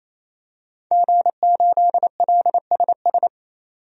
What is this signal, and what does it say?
Morse code, heard as a keyed tone, about 28 wpm, 700 Hz. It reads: G8LSS
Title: DXCC-100 Call Signs - 5 Characters - 28wpm